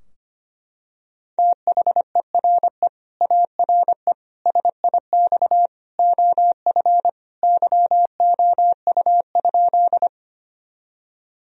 Morse code, heard as a keyed tone, about 25 wpm, 700 Hz. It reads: THERE ARE SIX OF YOU?
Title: Sentences from Top 1000 Words - Review Part 2 - 25wpm